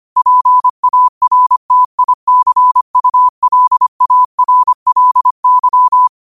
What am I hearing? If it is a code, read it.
Morse code, heard as a keyed tone, about 25 wpm, 1000 Hz. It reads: PARTICULARLY
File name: qotdfiles / 25wpm